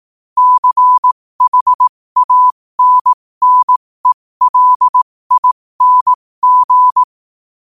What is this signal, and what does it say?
Morse code, heard as a keyed tone, about 18 wpm, spaced as for 15 wpm, 1000 Hz. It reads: CHANNELING
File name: qotdfiles / 15wpm